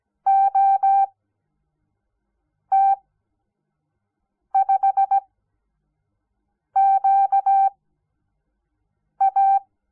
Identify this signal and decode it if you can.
Morse code, heard as a keyed tone, about 17 wpm, spaced as for 5 wpm, 780 Hz. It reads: OT5QA
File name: Onlinelehrgang / morsen / DJ4UF Morsen06